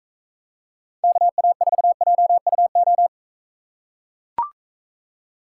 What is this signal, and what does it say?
Morse code, heard as a keyed tone, about 44 wpm, 700 Hz. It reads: KA4JUO